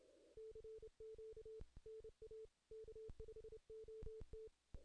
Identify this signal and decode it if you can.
Morse code, heard as a keyed tone, about 29 wpm, 440 Hz. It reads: CQ NA K5OT
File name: 2016 JAN CW NAQP / 2016 Jan naqp k5ot - 0-10